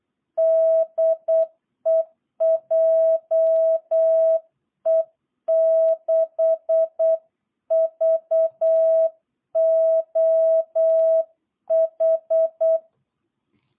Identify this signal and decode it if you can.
Morse code, heard as a keyed tone, about 8 wpm, 660 Hz. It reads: DE JE6VOH
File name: jh8cbh / 2022cw